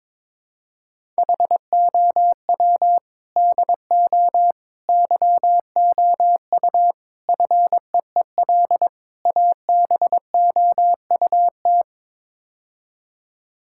Morse code, heard as a keyed tone, about 22 wpm, 700 Hz. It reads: HOW DO YOU FEEL ABOUT
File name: practice / 22wpm / Build A Phrase - 5-Word Phrases - 22wpm